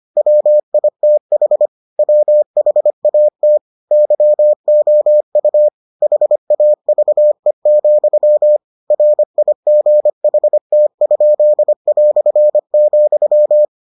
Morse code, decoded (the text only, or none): WITH WHAT YOU HAVE, RIGHT?",